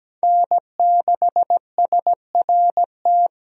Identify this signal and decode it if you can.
Morse code, heard as a keyed tone, about 17 wpm, 700 Hz. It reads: N6SRT
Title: USA Call Signs - 5 Characters - Repeated 2x - 17wpm